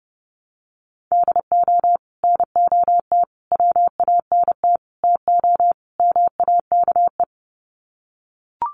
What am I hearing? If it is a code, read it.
Morse code, heard as a keyed tone, about 30 wpm, 700 Hz. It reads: DO NOT WANT TO MAKE
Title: Sets of 5 Words Encore 30wpm